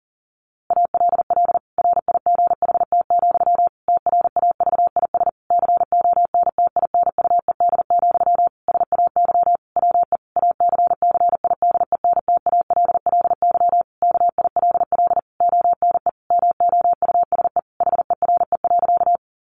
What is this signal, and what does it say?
Morse code, heard as a keyed tone, about 40 wpm, 700 Hz. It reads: ALL RIGHT, TRAVIS CONTINUED, SAY WE ACCIDENTALLY KILL ONE MOUSE HERE.
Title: A Sound Of Thunder 40wpm